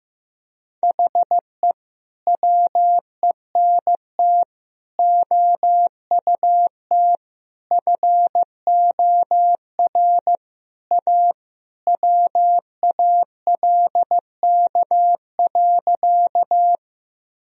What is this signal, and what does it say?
Morse code, heard as a keyed tone, about 14 wpm, 700 Hz. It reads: HE WENT OUT FOR A WALK.
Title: Sentences from Top 1000 Words - Review Part 1 - 15wpm